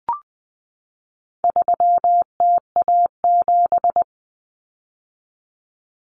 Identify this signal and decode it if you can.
Morse code, heard as a keyed tone, about 20 wpm, 700 Hz. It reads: 3TA7
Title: Common Cut Numbers - 4-Digit 20wpm